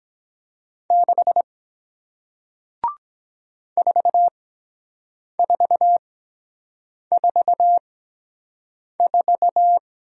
Morse code, heard as a keyed tone, about 23 wpm, spaced as for 7 wpm, 700 Hz. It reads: 6 4444